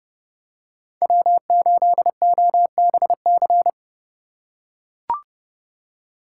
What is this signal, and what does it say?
Morse code, heard as a keyed tone, about 29 wpm, 700 Hz. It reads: W8OBC